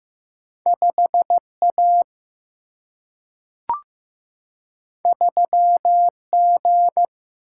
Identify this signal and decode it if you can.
Morse code, heard as a keyed tone, about 15 wpm, 700 Hz. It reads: 5A 3G
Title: Single-Letter Single-Number 15wpm